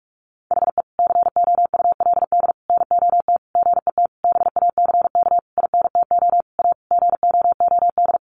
Morse code, heard as a keyed tone, about 44 wpm, 700 Hz. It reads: HE COULD NOT GET BACK INTO A GOOD
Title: Sentences from Top 1100 Words 45wpm